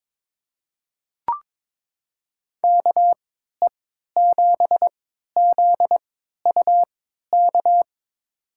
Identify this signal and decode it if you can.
Morse code, heard as a keyed tone, about 22 wpm, spaced as for 13 wpm, 700 Hz. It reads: KE7ZUK
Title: USA Call Signs - 3x Character Spacing - 22wpm